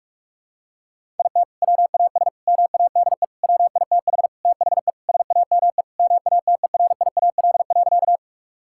Morse code, heard as a keyed tone, about 41 wpm, 690 Hz. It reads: IT WAS MADE WITH THE SAME MATERIAL.
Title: Sentences from Top 1000 Words - Review Part 5 - 45wpm